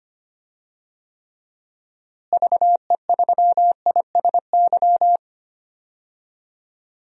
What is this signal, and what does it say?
Morse code, heard as a keyed tone, about 25 wpm, 700 Hz. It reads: VE3ISY
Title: DXCC-100 Call Signs - 6 Characters - Repeated 2x - 25wpm